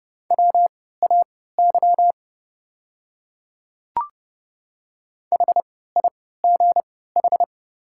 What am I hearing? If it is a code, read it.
Morse code, heard as a keyed tone, about 31 wpm, spaced as for 17 wpm, 700 Hz. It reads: WAY HIGH
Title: 3x Character Spacing - Top 200 Words - 30wpm